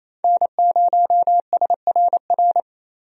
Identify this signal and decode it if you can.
Morse code, heard as a keyed tone, about 28 wpm, 700 Hz. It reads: N0SRR
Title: USA Call Signs - 5 Characters - Repeated 2x - 28wpm